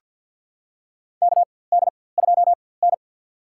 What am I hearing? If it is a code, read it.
Morse code, heard as a keyed tone, about 51 wpm, spaced as for 23 wpm, 700 Hz. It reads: KD2N
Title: USA Call Signs - 4x Character Spacing - 50wpm